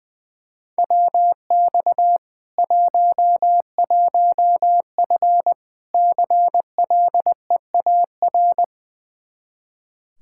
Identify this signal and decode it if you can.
Morse code, heard as a keyed tone, about 19 wpm, 700 Hz. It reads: WX 11F CLEAR
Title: Practice QSO - Exchange 2 - Info - 20wpm